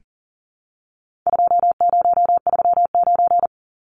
Morse code, heard as a keyed tone, about 40 wpm, 700 Hz. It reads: 2039
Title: Numbers - 4-Digit 40wpm